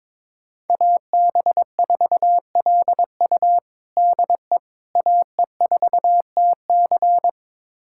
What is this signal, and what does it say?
Morse code, heard as a keyed tone, about 21 wpm, 700 Hz. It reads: AB4LU DE AE4TC
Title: USA Callsign DE Callsign - 5 Characters - 22wpm